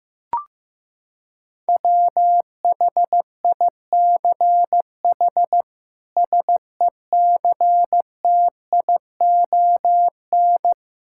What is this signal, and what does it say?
Morse code, heard as a keyed tone, about 15 wpm, 700 Hz. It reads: WHICH SECTION
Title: Sentences from Top 1000 Words - Review Part 5 - 15wpm